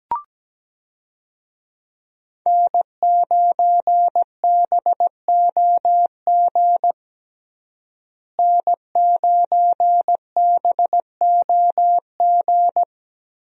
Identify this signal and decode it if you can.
Morse code, heard as a keyed tone, about 17 wpm, 700 Hz. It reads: N9BOG N9BOG
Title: USA Call Signs - 5 Characters - Repeated 3x - 17wpm